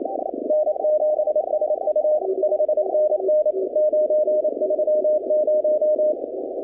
Morse code, heard as a keyed tone, about 29 wpm, 610 Hz, with noise. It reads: N7SU 5NN 930
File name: contests / 2019-cw-wpx-kt5j-k5pi- -2330 0-20